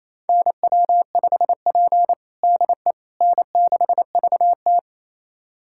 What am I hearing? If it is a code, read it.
Morse code, heard as a keyed tone, about 28 wpm, 690 Hz. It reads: NW5P DE N6VT
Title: USA Callsign DE Callsign - 4 Characters - 28wpm